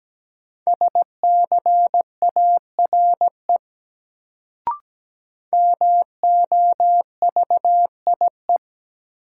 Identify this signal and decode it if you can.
Morse code, heard as a keyed tone, about 17 wpm, 700 Hz. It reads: SCARE MOVIE
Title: Sentences from Top 1000 Words - Review Part 5 - New Word Drill - 17wpm